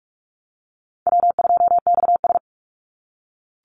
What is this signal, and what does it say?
Morse code, heard as a keyed tone, about 44 wpm, 700 Hz. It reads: W2XS
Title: Easy Call Signs - 45wpm